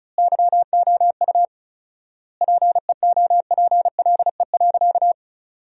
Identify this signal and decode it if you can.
Morse code, heard as a keyed tone, about 36 wpm, 700 Hz. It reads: YOU PEOPLE.
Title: 4x Word-Spacing - Sentences from Top 300 Words - 35wpm